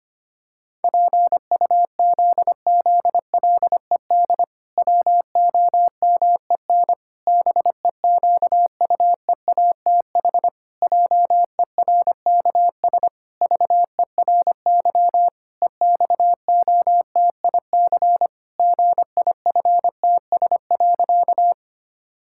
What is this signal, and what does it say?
Morse code, heard as a keyed tone, about 25 wpm, 700 Hz. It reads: PUZZLED WOMEN BEQUEATH JERKS VERY EXOTIC GIFTS.
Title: Pangram Challenge v2 - 25wpm